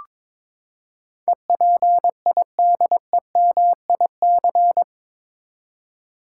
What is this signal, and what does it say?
Morse code, heard as a keyed tone, about 22 wpm, 700 Hz. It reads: EPIDEMIC
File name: practice / 22wpm / Eight Letter Words - 22wpm